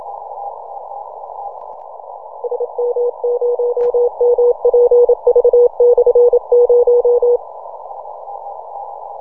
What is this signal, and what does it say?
Morse code, heard as a keyed tone, about 27 wpm, 480 Hz. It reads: SM0MPV/0
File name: simple-file-list / RI0KV AS27 20MCW